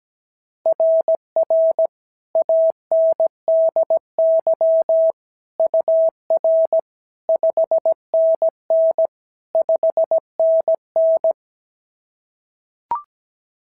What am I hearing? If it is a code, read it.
Morse code, heard as a keyed tone, about 17 wpm, 690 Hz. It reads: RR ANDY UR 5NN 5NN
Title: SOTA QSO Practice 17wpm